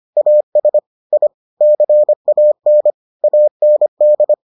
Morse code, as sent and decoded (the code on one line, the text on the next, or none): .- ... / .. / -.-. .- -. / .- -. -..
AS I CAN AND